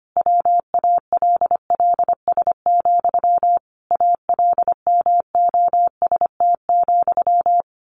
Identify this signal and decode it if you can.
Morse code, heard as a keyed tone, about 25 wpm, 700 Hz. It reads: WALLS, ALMOST,
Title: A Sound Of Thunder 25wpm